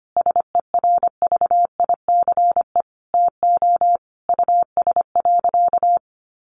Morse code, heard as a keyed tone, about 25 wpm, 700 Hz. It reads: SERVICE TO US.
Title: Sentences from Top 1200 Words 25wpm